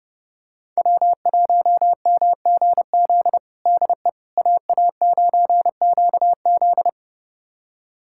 Morse code, heard as a keyed tone, about 30 wpm, 700 Hz. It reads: W1MGZ DE AA9QZ
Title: USA Callsign DE Callsign - 5 Characters - 30wpm